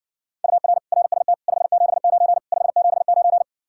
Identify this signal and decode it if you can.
Morse code, heard as a keyed tone, about 60 wpm, 700 Hz. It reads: UR RST 579 579